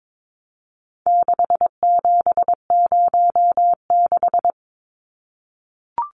Morse code, heard as a keyed tone, about 22 wpm, 700 Hz. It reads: 6706
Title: Numbers - 4-Digit 22wpm